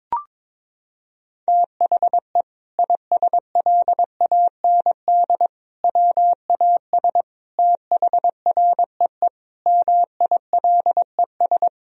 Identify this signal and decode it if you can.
Morse code, heard as a keyed tone, about 22 wpm, 700 Hz. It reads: THE ISLAND WAS THREE MILES
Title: Sentences from Top 1000 Words - Review Part 3 - 22wpm